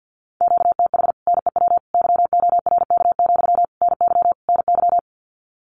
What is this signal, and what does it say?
Morse code, heard as a keyed tone, about 50 wpm, 700 Hz. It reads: QTH NEW YORK, NY NY
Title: Common QSO Phrases Encore 50wpm